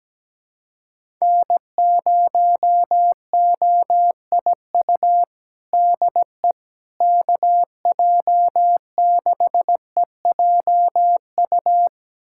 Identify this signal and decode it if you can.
Morse code, heard as a keyed tone, about 17 wpm, 700 Hz. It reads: N0OIU DE KJ6EJU
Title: USA Callsign DE Callsign 17wpm